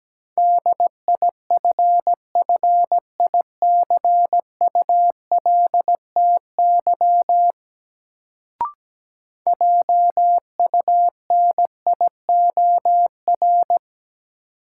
Morse code, heard as a keyed tone, about 17 wpm, 700 Hz. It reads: DIFFICULTY JUNIOR
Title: Sentences from Top 1500 Words - New Word Drill Alt - 17wpm